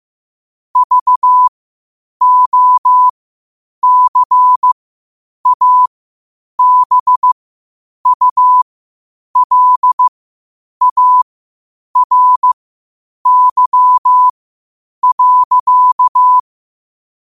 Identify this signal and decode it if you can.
Morse code, heard as a keyed tone, about 15 wpm, spaced as for 8 wpm, 1000 Hz. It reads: VOCABULARY.
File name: qotdfiles / 8wpm